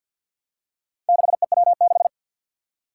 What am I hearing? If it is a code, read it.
Morse code, heard as a keyed tone, about 51 wpm, 700 Hz. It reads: 6EW6
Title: Four Challenging Letter-Number Combinations 50wpm